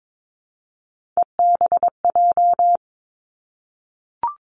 Morse code, heard as a keyed tone, about 22 wpm, 700 Hz. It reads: EBJ
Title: Three Challenging Letter-Number Combinations 22wpm